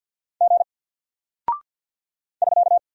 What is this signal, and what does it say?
Morse code, heard as a keyed tone, about 48 wpm, 700 Hz. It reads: G <SK>
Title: Single Character (Letter:Number:Punctuation:Prosign) 50wpm